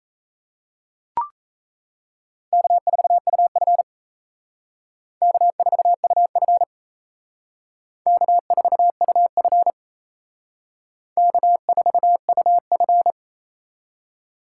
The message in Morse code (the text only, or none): K4UF K4UF K4UF K4UF